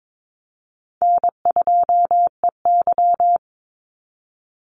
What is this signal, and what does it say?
Morse code, heard as a keyed tone, about 22 wpm, 700 Hz. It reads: N2EY
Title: Call Signs Repeated 3x - 22wpm